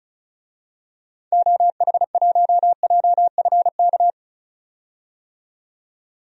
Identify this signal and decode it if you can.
Morse code, heard as a keyed tone, about 34 wpm, 700 Hz. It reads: OH1JFK
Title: DXCC-100 Call Signs - 6 Characters - 35wpm